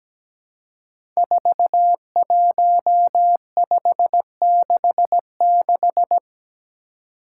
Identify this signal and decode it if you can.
Morse code, heard as a keyed tone, about 17 wpm, 700 Hz. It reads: 41566
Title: Five Challenging Letter-Number Combinations 17wpm